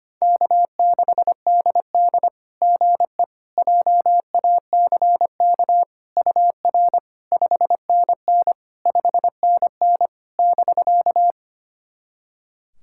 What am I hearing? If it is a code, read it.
Morse code, heard as a keyed tone, about 25 wpm, 700 Hz. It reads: K6DD GE JACK UR 5NN 5NN <BK>
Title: SOTA QSO Practice 25wpm